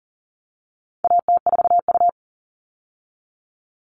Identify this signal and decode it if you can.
Morse code, heard as a keyed tone, about 40 wpm, 700 Hz. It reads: AT4U
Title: All Cut Numbers - 4-Digit 40wpm